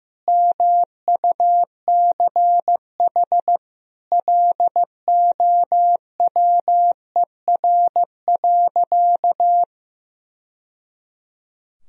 Morse code, heard as a keyed tone, about 15 wpm, 700 Hz. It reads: MUCH LOWER.
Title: Sentences from Top 1000 Words - Review Part 4 - 15wpm